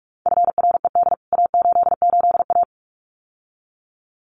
Morse code, heard as a keyed tone, about 44 wpm, 700 Hz. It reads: FRED A88A
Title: CWT Exchange 45wpm